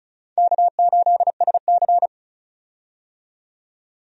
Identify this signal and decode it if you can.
Morse code, heard as a keyed tone, about 36 wpm, 700 Hz. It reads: K8SC
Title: USA Call Signs - 4 Characters - 35wpm